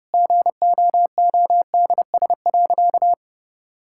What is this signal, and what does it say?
Morse code, heard as a keyed tone, about 30 wpm, 700 Hz. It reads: GOODS.
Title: Sentences from Top 1000 Words - Review Part 6 - 30wpm